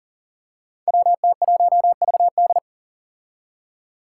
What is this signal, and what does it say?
Morse code, heard as a keyed tone, about 42 wpm, 700 Hz. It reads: WT1VD
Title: Five Challenging Letter-Number Combinations 40wpm